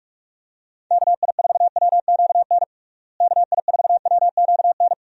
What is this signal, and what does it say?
Morse code, heard as a keyed tone, about 42 wpm, 700 Hz. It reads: KI4WQN KI4WQN
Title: USA Call Signs - 6 Characters - Repeated 2x - 45wpm